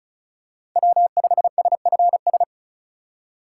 Morse code, heard as a keyed tone, about 36 wpm, 700 Hz. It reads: W5SFS
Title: USA Call Signs - 5 Characters - 3x Character Spacing - 35wpm